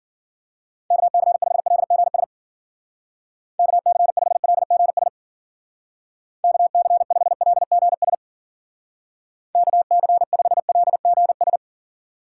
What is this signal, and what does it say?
Morse code, heard as a keyed tone, about 49 wpm, 700 Hz. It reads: KC5LGS KC5LGS KC5LGS KC5LGS
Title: Speed Racing - USA 2x3 Call Signs 40wpm